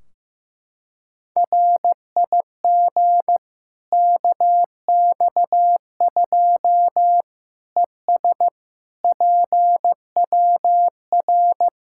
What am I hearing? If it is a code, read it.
Morse code, heard as a keyed tone, about 15 wpm, 700 Hz. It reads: RIG KX2 ES PWR
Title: K1USN SST QSO Practice 15wpm